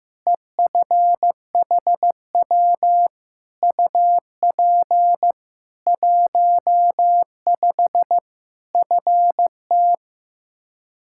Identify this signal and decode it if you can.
Morse code, heard as a keyed tone, about 15 wpm, 700 Hz. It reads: EFHW UP 15 FT
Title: Practice QSO - Exchange 2 - Info - 15wpm